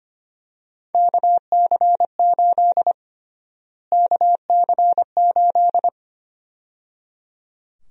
Morse code, heard as a keyed tone, about 25 wpm, 700 Hz. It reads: KC8 KC8
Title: USA Call Sign Prefixes - Repeated 2x - 25wpm